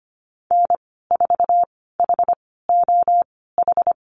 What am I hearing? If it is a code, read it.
Morse code, heard as a keyed tone, about 25 wpm, spaced as for 16 wpm, 700 Hz. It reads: N4HOH
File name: Lesson 15g - Number 5 - Call Signs 25wpm